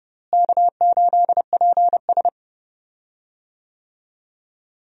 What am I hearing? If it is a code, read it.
Morse code, heard as a keyed tone, about 31 wpm, 700 Hz. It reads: K8PS
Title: USA Call Signs - 4 Characters - Repeated 3x - 30wpm